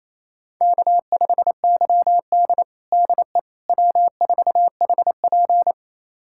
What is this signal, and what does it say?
Morse code, heard as a keyed tone, about 28 wpm, 700 Hz. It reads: K5YD DE W4HP
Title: USA Callsign DE Callsign - 4 Characters - 28wpm